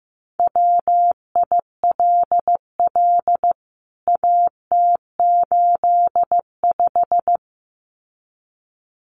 Morse code, heard as a keyed tone, about 15 wpm, 700 Hz. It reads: WILL AT85
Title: CWT Exchange 15wpm